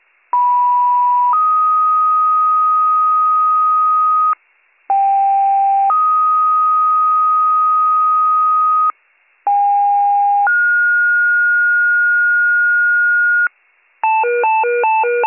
Two-tone page, heard as a one-second tone, then a three-second tone, about 980 then 1250 Hz, about 790 then 1250 Hz, about 810 then 1430 Hz. It is heard from 0.3 to 13.5 s.